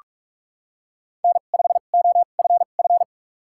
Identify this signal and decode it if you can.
Morse code, heard as a keyed tone, about 46 wpm, spaced as for 32 wpm, 700 Hz. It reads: N5OFF
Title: USA Call Signs - 2x Character Spacing - 45wpm